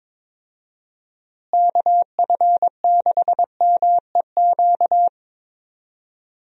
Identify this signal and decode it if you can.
Morse code, heard as a keyed tone, about 21 wpm, 700 Hz. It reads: KF6MEQ
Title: USA Call Signs - 6 Characters - Repeated 2x - 22wpm